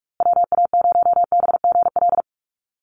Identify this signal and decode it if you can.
Morse code, heard as a keyed tone, about 44 wpm, 700 Hz. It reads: WA0BGL
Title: USA Call Signs Repeated 3x 45wpm